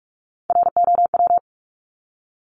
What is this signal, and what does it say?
Morse code, heard as a keyed tone, about 44 wpm, 700 Hz. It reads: ROW
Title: Top 1751-2000 Words 45wpm